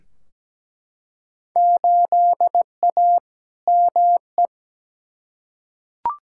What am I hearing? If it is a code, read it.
Morse code, heard as a keyed tone, about 17 wpm, 700 Hz. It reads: 8A ME
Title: Field Day Exchanges - 17wpm